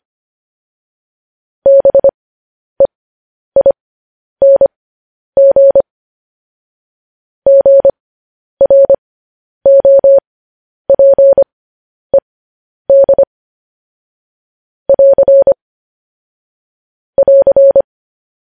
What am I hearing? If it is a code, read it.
Morse code, heard as a keyed tone, about 25 wpm, spaced as for 10 wpm, 550 Hz. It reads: BEING GROPED <AR> <AR>